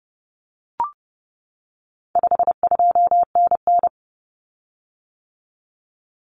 Morse code, heard as a keyed tone, about 30 wpm, 700 Hz. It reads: 52NN
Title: Common Cut Numbers - 4-Digit 30wpm